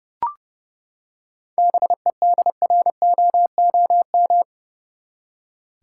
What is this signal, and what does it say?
Morse code, heard as a keyed tone, about 30 wpm, 700 Hz. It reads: BEDROOM
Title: Seven Letter Words - 30wpm